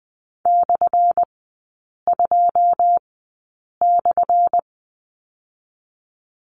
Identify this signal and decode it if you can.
Morse code, heard as a keyed tone, about 20 wpm, spaced as for 8 wpm, 700 Hz. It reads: /2/
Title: 178 Lesson 29e - Slash - Three Characters 20wpm